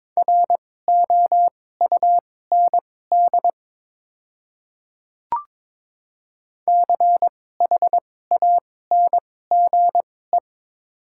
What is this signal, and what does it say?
Morse code, heard as a keyed tone, about 23 wpm, spaced as for 16 wpm, 700 Hz. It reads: ROUND CHANGE
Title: 2x Character Spacing - Top 300 Words - 22wpm